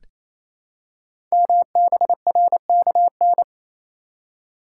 Morse code, heard as a keyed tone, about 27 wpm, 700 Hz. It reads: MBRKN
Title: Code Groups - 5 Letters - 28wpm